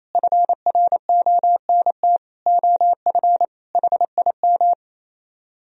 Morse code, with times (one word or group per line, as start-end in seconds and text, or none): 0.1-2.2 FRONT
2.5-3.5 OF
3.8-4.7 HIM